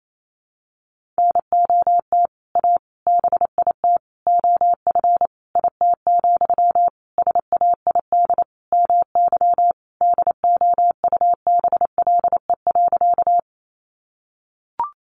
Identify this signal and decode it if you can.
Morse code, heard as a keyed tone, about 28 wpm, 700 Hz. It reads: NOT A BIT OF IT, SAID MY DOUBLE.
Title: Sentences from Top 1400 Words 28wpm